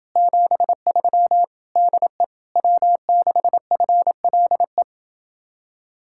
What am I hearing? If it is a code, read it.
Morse code, heard as a keyed tone, about 30 wpm, 700 Hz. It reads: 73 DE W6FLE